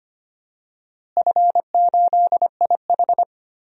Morse code, heard as a keyed tone, about 25 wpm, 700 Hz. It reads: F8IH
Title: DXCC-100 Call Signs - 4 Characters - Repeated 2x - 25wpm